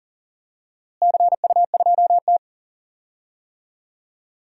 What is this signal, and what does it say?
Morse code, heard as a keyed tone, about 40 wpm, 700 Hz. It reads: CU2T
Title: DXCC-100 Call Signs - 4 Characters - Repeated 3x - 40wpm